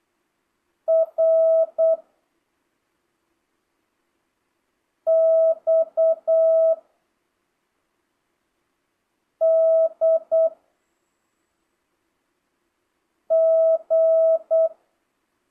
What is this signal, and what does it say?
Morse code, heard as a keyed tone, about 8 wpm, 650 Hz. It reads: R X D G
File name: jh8cbh / 2025cw